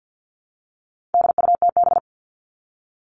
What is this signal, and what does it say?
Morse code, heard as a keyed tone, about 50 wpm, 700 Hz. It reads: DUTB
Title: All Cut Numbers - 4-Digit 50wpm